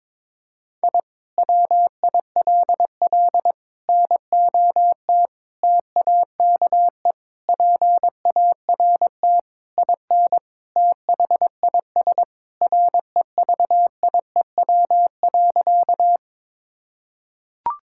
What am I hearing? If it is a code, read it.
Morse code, heard as a keyed tone, about 22 wpm, 700 Hz. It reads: I WILL NOT TAKE PART IN THIS REVIEW.
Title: Sentences from Top 1000 Words - Review Part 6 - 22wpm